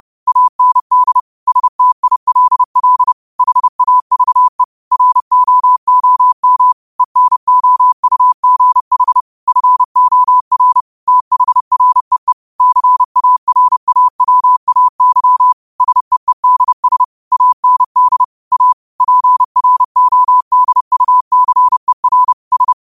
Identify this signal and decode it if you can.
Morse code, heard as a keyed tone, about 30 wpm, 1000 Hz. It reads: AND STILL HAVE ROOM ENOUGH FOR THREE CARAWAY SEEDS AND A PRODUCER S